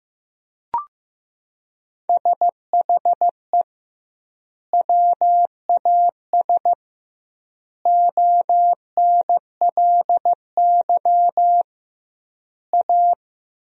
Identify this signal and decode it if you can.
Morse code, heard as a keyed tone, about 15 wpm, 700 Hz. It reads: SHE WAS ONLY A
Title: 2x Word-Spacing - Sentences from Top 400 Words - 15wpm